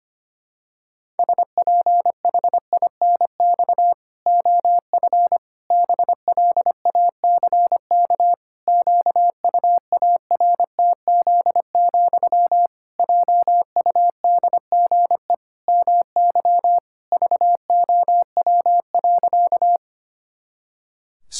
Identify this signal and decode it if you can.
Morse code, heard as a keyed tone, about 25 wpm, 700 Hz. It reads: SPHINX OF BLACK QUARTZ, JUDGE MY VOW.
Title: Pangram Challenge v2 - 25wpm